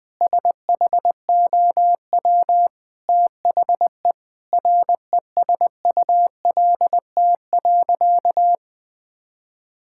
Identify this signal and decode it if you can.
Morse code, heard as a keyed tone, about 20 wpm, 700 Hz. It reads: SHOW THE RESULT.
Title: Sentences from Top 1000 Words - Review Part 5 - 20wpm